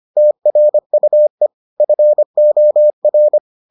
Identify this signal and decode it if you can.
Morse code, heard as a keyed tone, about 25 wpm, 600 Hz. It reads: TRUE FOR